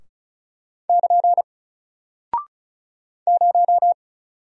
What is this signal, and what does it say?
Morse code, heard as a keyed tone, about 36 wpm, 700 Hz. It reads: <KN> 0
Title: Single Character (Letter:Number:Punctuation:Prosign) Alt 35wpm